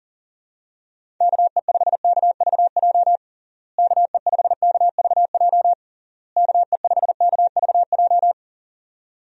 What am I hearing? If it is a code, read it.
Morse code, heard as a keyed tone, about 37 wpm, 700 Hz. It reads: KE5KVJ KE5KVJ KE5KVJ
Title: USA Call Signs - 6 Characters - Repeated 3x - 40wpm